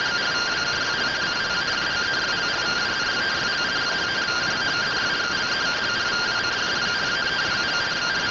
Two-tone teleprinter signal, about 170 Hz shift.